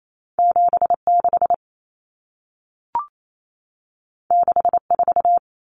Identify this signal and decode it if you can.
Morse code, heard as a keyed tone, about 28 wpm, 700 Hz. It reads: 76 64
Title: Numbers - 2-Digit 28wpm